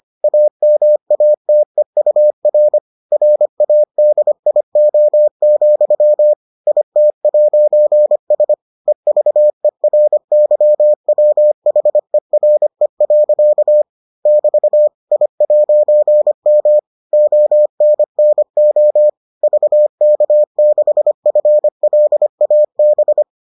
Morse code, heard as a keyed tone, about 25 wpm, 600 Hz. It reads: AMATEUR RADIO, IT'S EVERYWHERE. <BT> I'M ONNO VK6FLAB